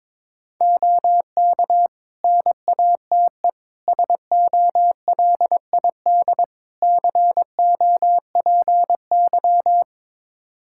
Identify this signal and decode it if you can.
Morse code, heard as a keyed tone, about 22 wpm, 700 Hz. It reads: OK NATE SOLID COPY
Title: Practice QSO - Exchange 3 - More Info - 22wpm